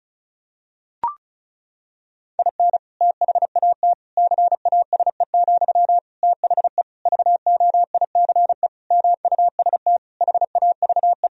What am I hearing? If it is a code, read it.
Morse code, heard as a keyed tone, about 36 wpm, 700 Hz. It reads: IN THAT CASE, THE VOICE MUST HAVE
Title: Sentences from Top 1000 Words - Review Part 2 - 35wpm